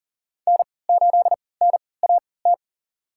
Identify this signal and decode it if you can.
Morse code, heard as a keyed tone, about 39 wpm, spaced as for 22 wpm, 700 Hz. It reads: N8NAT
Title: USA Call Signs - 5 Characters - 3x Character Spacing - 40wpm